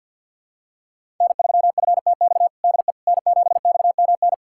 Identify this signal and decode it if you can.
Morse code, heard as a keyed tone, about 50 wpm, 700 Hz. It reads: N3FTX DE N7XMN
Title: USA Callsign DE Callsign - 5 Characters - 50wpm